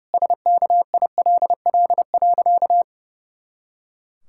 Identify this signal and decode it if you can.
Morse code, heard as a keyed tone, about 32 wpm, 700 Hz. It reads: SKILL.
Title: Pangram Challenge v2 - 30wpm